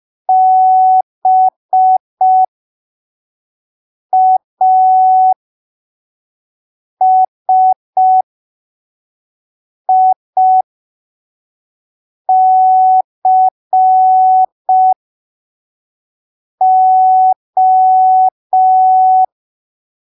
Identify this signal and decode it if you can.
Morse code, heard as a keyed tone, about 5 wpm, spaced as for 3 wpm, 750 Hz. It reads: BASICO
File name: arquivos / texto / texto 09 05ppm